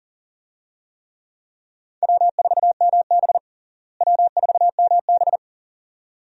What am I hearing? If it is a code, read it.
Morse code, heard as a keyed tone, about 40 wpm, 700 Hz. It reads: W4MB W4MB
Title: USA Call Signs - 4 Characters - Repeated 2x - 40wpm